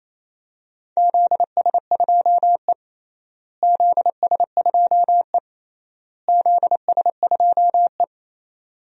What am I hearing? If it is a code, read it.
Morse code, heard as a keyed tone, about 28 wpm, 700 Hz. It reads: ZS2E ZS2E ZS2E
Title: DXCC-100 Call Signs - 4 Characters - Repeated 3x - 28wpm